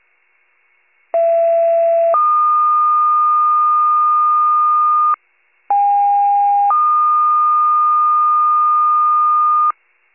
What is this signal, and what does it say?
Two-tone page, heard as a one-second tone, then a three-second tone, about 670 then 1220 Hz, about 810 then 1220 Hz. It is heard from 1.1 to 9.7 s.